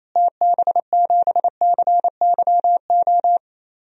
Morse code, heard as a keyed tone, about 28 wpm, 700 Hz. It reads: TB7CYO
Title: DXCC-100 Call Signs - 6 Characters - Repeated 3x - 28wpm